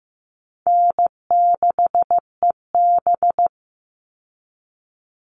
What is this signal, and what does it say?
Morse code, heard as a keyed tone, about 15 wpm, 700 Hz. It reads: N6EB
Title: All Cut Numbers - 4-Digit 15wpm